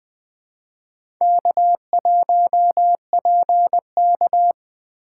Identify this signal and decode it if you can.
Morse code, heard as a keyed tone, about 20 wpm, 700 Hz. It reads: K1PK
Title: USA Call Signs - 3x Character Spacing - 20wpm